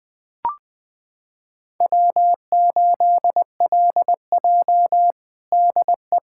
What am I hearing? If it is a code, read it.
Morse code, heard as a keyed tone, about 20 wpm, 700 Hz. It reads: W8LJ DE
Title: USA Callsign DE Callsign - 4 Characters - 20wpm